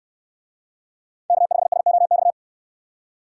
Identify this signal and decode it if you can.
Morse code, heard as a keyed tone, about 69 wpm, 700 Hz. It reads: K5SOX